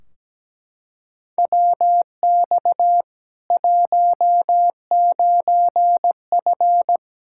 Morse code, heard as a keyed tone, about 17 wpm, 700 Hz. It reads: WX 19F
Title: K1USN SST QSO Practice 17wpm